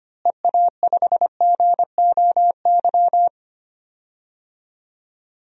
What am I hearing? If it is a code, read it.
Morse code, heard as a keyed tone, about 25 wpm, 700 Hz. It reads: EA5GOY
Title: DXCC-100 Call Signs - 6 Characters - Repeated 3x - 25wpm